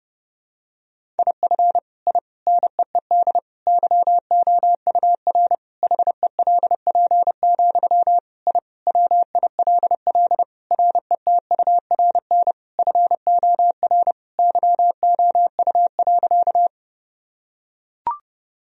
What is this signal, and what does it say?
Morse code, heard as a keyed tone, about 30 wpm, 700 Hz. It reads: IF I NEED YOUR HELP, I WILL RETURN FOR YOU.
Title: Sentences from Top 1000 Words - Review Part 1 - 30wpm